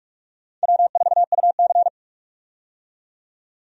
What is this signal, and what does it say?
Morse code, heard as a keyed tone, about 46 wpm, 700 Hz. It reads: W4UC
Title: USA Call Signs - 4 Characters - 45wpm